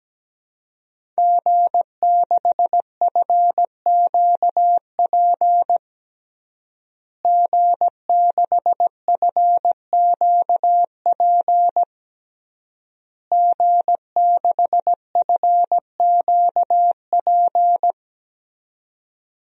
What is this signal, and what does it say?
Morse code, heard as a keyed tone, about 17 wpm, 700 Hz. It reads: G6FQP G6FQP G6FQP
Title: DXCC-100 Call Signs - 5 Characters - Repeated 3x - 17wpm